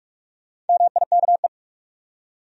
Morse code, heard as a keyed tone, about 43 wpm, 700 Hz. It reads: MIKE
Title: USA Common Names - Max 4 Letters - 45wpm